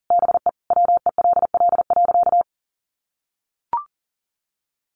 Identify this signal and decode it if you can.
Morse code, heard as a keyed tone, about 40 wpm, 700 Hz. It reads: BE WELL.
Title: Sentences from Top 1500 Words 40wpm